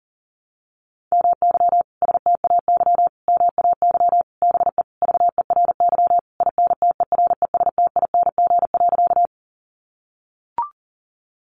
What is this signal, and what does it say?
Morse code, heard as a keyed tone, about 40 wpm, 700 Hz. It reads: MY STAY MAY BE VERY INTERESTING.
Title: Sentences from Top 1000 Words 40wpm